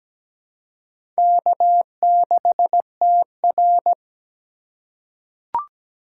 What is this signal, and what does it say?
Morse code, heard as a keyed tone, about 17 wpm, 700 Hz. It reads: K6TR
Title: USA Call Signs - 4 Characters - 4x Character Spacing - 17wpm